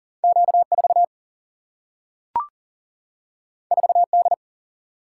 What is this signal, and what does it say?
Morse code, heard as a keyed tone, about 40 wpm, 700 Hz. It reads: Q 4 4 D
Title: Single-Letter Single-Number 40wpm